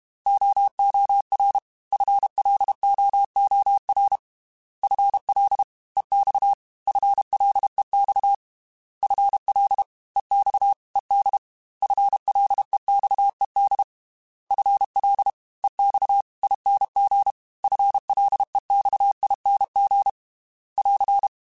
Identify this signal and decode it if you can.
Morse code, heard as a keyed tone, about 32 wpm, 800 Hz. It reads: OOR FLOOR FL EX FLEX FL EX ED FLEXED FL EX ING FLEXING <AR>
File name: FL / FL mp3s / Fast FL @30